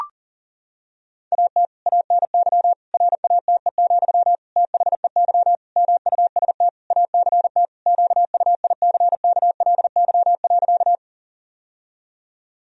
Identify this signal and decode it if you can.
Morse code, heard as a keyed tone, about 37 wpm, 700 Hz. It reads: AT ANY RATE, THEY MUST ACT QUICKLY.
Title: Sentences from Top 1000 Words - Review Part 4 - 40wpm